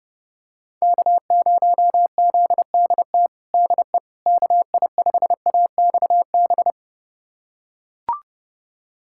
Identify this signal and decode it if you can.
Morse code, heard as a keyed tone, about 30 wpm, 700 Hz. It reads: K0ZDT DE KI5AXB
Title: USA Callsign DE Callsign 30wpm